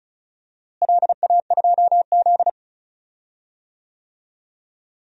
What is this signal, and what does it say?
Morse code, heard as a keyed tone, about 36 wpm, 700 Hz. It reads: LA2Z
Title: DXCC-100 Call Signs - 4 Characters - Repeated 3x - 35wpm